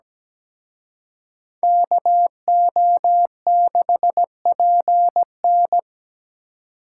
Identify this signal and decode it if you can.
Morse code, heard as a keyed tone, about 20 wpm, 700 Hz. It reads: KO6PN
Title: Speed Racing - USA 2x2 Call Signs 17wpm